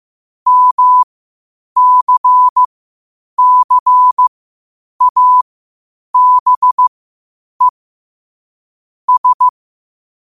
Morse code, heard as a keyed tone, about 15 wpm, spaced as for 8 wpm, 1000 Hz. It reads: MCCABE S
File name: qotdfiles / 8wpm